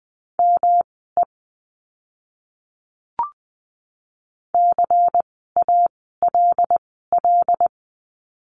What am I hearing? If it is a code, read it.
Morse code, heard as a keyed tone, about 20 wpm, spaced as for 14 wpm, 700 Hz. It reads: ME CALL
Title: 230 Lesson 37f - Letter x - Words 20wpm